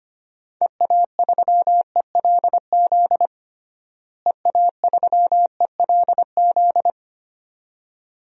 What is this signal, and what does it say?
Morse code, heard as a keyed tone, about 25 wpm, 700 Hz. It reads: EA3ELZ EA3ELZ
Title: DXCC-100 Call Signs - 6 Characters - Repeated 2x - 25wpm